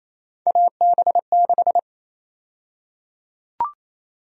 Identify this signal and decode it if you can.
Morse code, heard as a keyed tone, about 28 wpm, 700 Hz. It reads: AB6
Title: USA Call Sign Prefixes - Repeated 2x - 28wpm